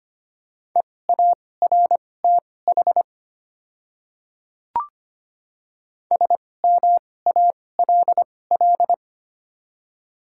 Morse code, heard as a keyed tone, about 25 wpm, spaced as for 18 wpm, 700 Hz. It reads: EARTH SMALL
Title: 2x Character Spacing - Top 200 Words - 25wpm